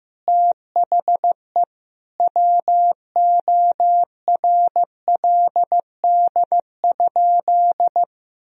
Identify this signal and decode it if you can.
Morse code, heard as a keyed tone, about 16 wpm, 700 Hz. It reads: THE WORLD?
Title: Sentences from Top 200 Words and Abbreviations 15wpm